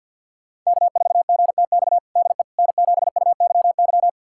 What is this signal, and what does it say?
Morse code, heard as a keyed tone, about 50 wpm, 700 Hz. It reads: K4GTX DE N7UYY
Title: USA Callsign DE Callsign 50wpm